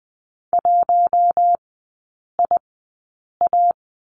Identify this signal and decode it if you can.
Morse code, heard as a keyed tone, about 20 wpm, spaced as for 8 wpm, 700 Hz. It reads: 1IA
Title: 053 Lesson 10e - Letter r - Three Characters 20wpm